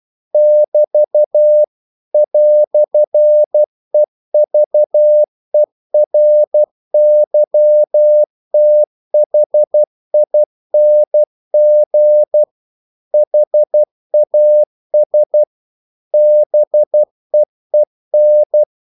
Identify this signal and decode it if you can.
Morse code, heard as a keyed tone, about 12 wpm, 600 Hz. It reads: <BT> "EVERYTHING HAS BEEN
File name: CW / Practice / EnglishQuotations0312